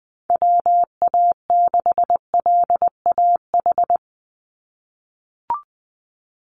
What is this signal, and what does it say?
Morse code, heard as a keyed tone, about 20 wpm, 700 Hz. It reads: WA6LAH
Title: USA Call Signs 20wpm